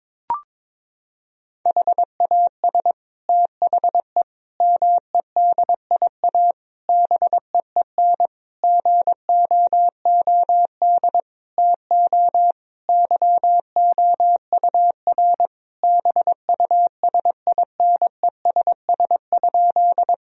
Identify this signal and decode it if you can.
Morse code, heard as a keyed tone, about 22 wpm, 700 Hz. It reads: HAS THE MEDIA BEEN GOOD TO YOUR BUSINESS?
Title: Sentences from Top 1000 Words - Review Part 4 - 22wpm